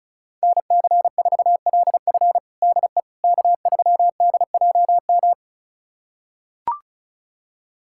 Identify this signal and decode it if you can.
Morse code, heard as a keyed tone, about 36 wpm, 700 Hz. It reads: NC4LF DE K3DJM
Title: USA Callsign DE Callsign - 5 Characters - 35wpm